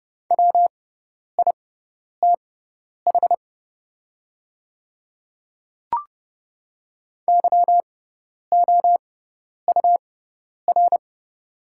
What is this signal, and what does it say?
Morse code, heard as a keyed tone, about 30 wpm, spaced as for 10 wpm, 700 Hz. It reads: WITH YOUR
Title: 6x Character Spacing - Top 100 Words - 30wpm